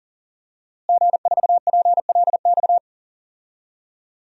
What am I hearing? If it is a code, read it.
Morse code, heard as a keyed tone, about 40 wpm, 700 Hz. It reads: G4PLX
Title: DXCC-100 Call Signs - 5 Characters - Repeated 3x - 40wpm